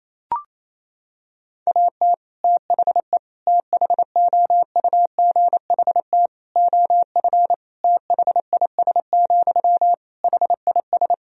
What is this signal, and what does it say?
Morse code, heard as a keyed tone, about 28 wpm, 700 Hz. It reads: AT THE THOUGHT OF THIS, HIS
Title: Sentences from Top 1000 Words - Review Part 2 - 28wpm